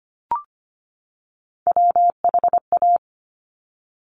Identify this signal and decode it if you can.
Morse code, heard as a keyed tone, about 25 wpm, 700 Hz. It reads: WHA
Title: Top Three Letter Combinations in Words 25wpm